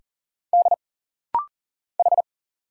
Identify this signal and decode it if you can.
Morse code, heard as a keyed tone, about 40 wpm, 700 Hz. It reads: D H
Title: Single Letters - Rapid-Fire 40wpm